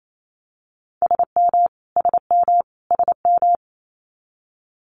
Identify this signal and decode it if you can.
Morse code, heard as a keyed tone, about 28 wpm, 700 Hz. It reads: SM SM SM
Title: Multi-letter Phonemes Repeated 3x 28wpm